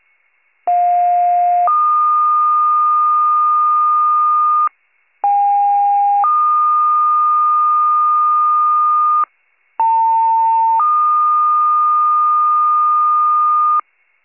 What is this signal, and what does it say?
Two-tone page, heard as a one-second tone, then a three-second tone, about 710 then 1220 Hz, about 810 then 1220 Hz, about 900 then 1220 Hz. It is heard from 0.7 to 13.8 s.